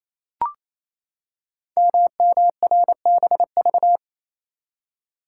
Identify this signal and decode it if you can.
Morse code, heard as a keyed tone, about 29 wpm, 700 Hz. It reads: MMRBV
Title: Code Groups - 5 Letters - 28wpm